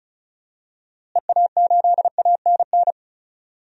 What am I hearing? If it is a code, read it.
Morse code, heard as a keyed tone, about 34 wpm, 700 Hz. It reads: EA8ANN
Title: DXCC-100 Call Signs - 6 Characters - Repeated 2x - 35wpm